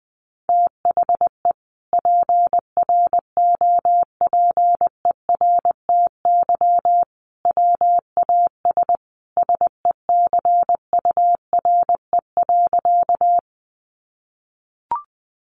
Morse code, heard as a keyed tone, about 20 wpm, 700 Hz. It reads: THE PROPERTY WAS SECURE.
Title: Sentences from Top 1200 Words 20wpm